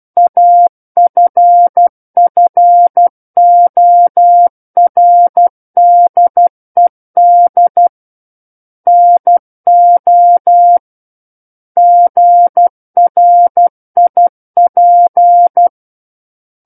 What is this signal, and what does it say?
Morse code, heard as a keyed tone, about 12 wpm, 700 Hz. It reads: AFFORDED NO GRIP
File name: Book One / 03-Chapter 3 - 12 WPM